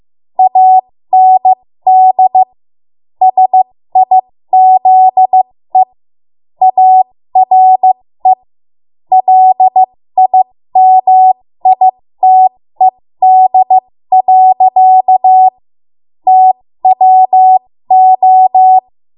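Morse code, heard as a keyed tone, about 15 wpm, spaced as for 13 wpm, 750 Hz. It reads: AND SIZE ARE LIMITED. TWO
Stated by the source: 13 wpm Morse code practice archive transmission.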